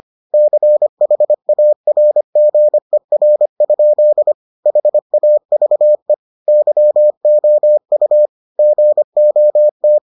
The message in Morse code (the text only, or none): CHARGER? HAVE YOU GOT